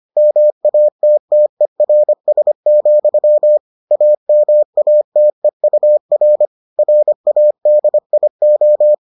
Morse code, heard as a keyed tone, about 25 wpm, 600 Hz. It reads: MATTERS, AMATEUR RADIO